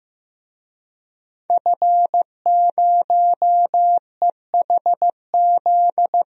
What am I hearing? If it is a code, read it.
Morse code, heard as a keyed tone, about 15 wpm, 700 Hz. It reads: F0EHZ
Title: DXCC-100 Call Signs - 5 Characters - 15wpm